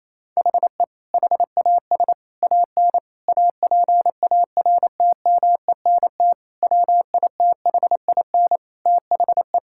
Morse code, heard as a keyed tone, about 28 wpm, 700 Hz. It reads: HE HAS AN APARTMENT WITHIN THE
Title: Sentences from Top 1000 Words - Review Part 6 - 28wpm